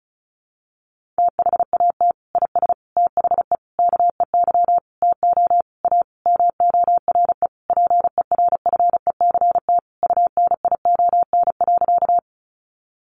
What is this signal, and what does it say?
Morse code, heard as a keyed tone, about 34 wpm, 700 Hz. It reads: THAT IS THE KEY TO A MORE PERFECT UNION.
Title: Sentences from Top 1400 Words 35wpm